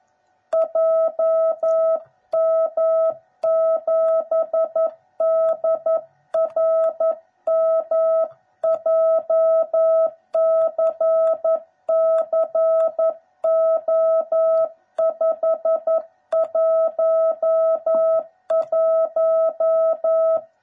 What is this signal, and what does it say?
Morse code, heard as a keyed tone, about 11 wpm, 660 Hz. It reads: JM7DRM JCCO511
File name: jh8cbh / 2021cw / cw09b